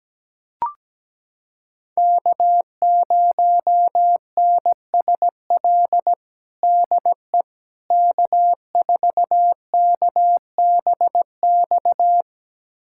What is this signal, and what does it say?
Morse code, heard as a keyed tone, about 17 wpm, 700 Hz. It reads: K0NSL DE K4KBX
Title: USA Callsign DE Callsign - 5 Characters - 17wpm